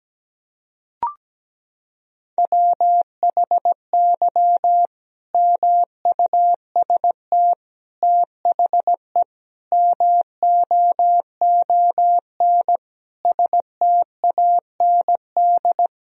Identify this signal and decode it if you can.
Morse code, heard as a keyed tone, about 17 wpm, 700 Hz. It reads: WHY MUST THE MOON STAND
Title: Sentences from Top 1000 Words - Review Part 3 - 17wpm